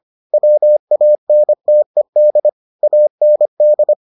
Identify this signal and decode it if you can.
Morse code, heard as a keyed tone, about 25 wpm, 600 Hz. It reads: WANTED AND